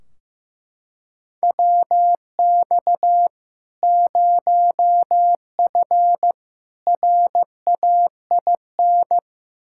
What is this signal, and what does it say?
Morse code, heard as a keyed tone, about 15 wpm, 700 Hz. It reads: WX 0F RAIN
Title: Practice QSO - Exchange 2 - Info - 15wpm